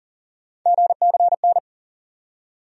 Morse code, heard as a keyed tone, about 40 wpm, 700 Hz. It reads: GCN
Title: USA Call Sign Postfixes - 3 Letters 40wpm